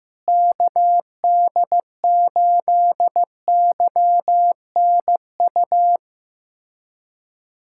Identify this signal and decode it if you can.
Morse code, heard as a keyed tone, about 15 wpm, 700 Hz. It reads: KD8YNU
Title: USA Callsign DE Callsign - 6 Characters - 15wpm